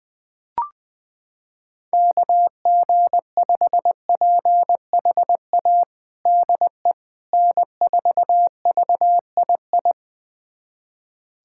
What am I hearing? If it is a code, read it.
Morse code, heard as a keyed tone, about 20 wpm, 700 Hz. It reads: KG5PHA DE N4VII
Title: USA Callsign DE Callsign 20wpm